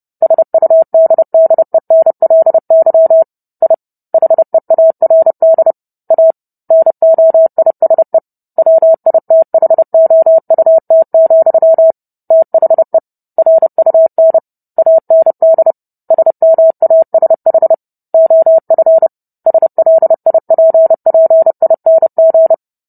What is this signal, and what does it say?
Morse code, heard as a keyed tone, about 30 wpm, 650 Hz. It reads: SUDDENLY I HEARD A NOISE WITHOUT, THE RUN AND SMASH OF SLIPPING